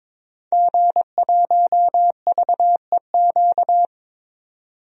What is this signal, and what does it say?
Morse code, heard as a keyed tone, about 21 wpm, 700 Hz. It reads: G1VEQ
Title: DXCC-100 Call Signs - 5 Characters - 22wpm